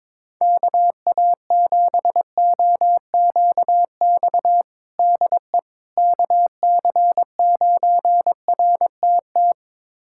Morse code, heard as a keyed tone, about 21 wpm, 700 Hz. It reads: KA7OQX DE KC9RTT
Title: USA Callsign DE Callsign - 6 Characters - 22wpm